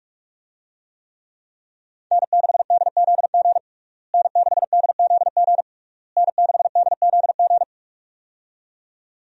Morse code, heard as a keyed tone, about 46 wpm, 700 Hz. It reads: N6DZG N6DZG N6DZG
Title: USA Call Signs - 5 Characters - Repeated 3x - 45wpm